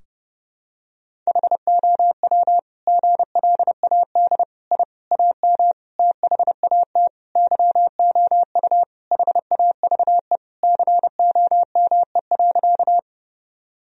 Morse code, heard as a keyed tone, about 32 wpm, 700 Hz. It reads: HOW GLAD I AM THAT YOU HAVE COME.
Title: Sentences from Top 1000 Words - Review Part 2 - 30wpm